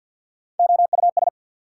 Morse code, heard as a keyed tone, about 54 wpm, 690 Hz. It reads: OUS
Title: Prefixes and Suffixes 50wpm